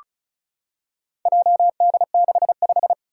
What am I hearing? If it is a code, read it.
Morse code, heard as a keyed tone, about 36 wpm, 690 Hz. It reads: JD65
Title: Four Challenging Letter-Number Combinations 35wpm